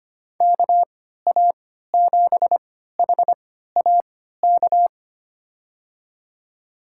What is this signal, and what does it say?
Morse code, heard as a keyed tone, about 25 wpm, spaced as for 14 wpm, 700 Hz. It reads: KA7HAK
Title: USA Call Signs - 6 Characters - 3x Character Spacing - 25wpm